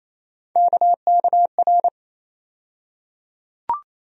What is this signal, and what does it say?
Morse code, heard as a keyed tone, about 28 wpm, 700 Hz. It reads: KKR
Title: USA Call Sign Postfixes - 3 Letters 28wpm